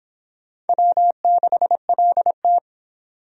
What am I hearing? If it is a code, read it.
Morse code, heard as a keyed tone, about 24 wpm, 700 Hz. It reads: W6LT